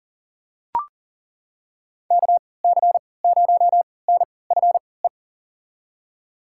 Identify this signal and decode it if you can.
Morse code, heard as a keyed tone, about 40 wpm, spaced as for 23 wpm, 700 Hz. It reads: KC0NFE
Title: USA Call Signs - 6 Characters - 3x Character Spacing - 40wpm